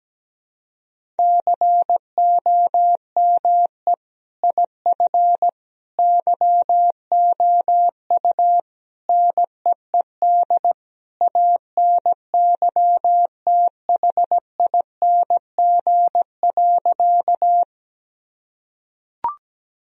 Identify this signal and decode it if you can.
Morse code, heard as a keyed tone, about 17 wpm, 700 Hz. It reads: COME IF YOU NEED ANYTHING.
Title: Sentences from Top 1000 Words - Review Part 1 - 17wpm